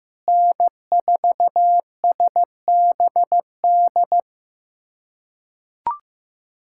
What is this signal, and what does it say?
Morse code, heard as a keyed tone, about 15 wpm, 700 Hz. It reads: N4SBD
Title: USA Callsign DE Callsign - 5 Characters - 15wpm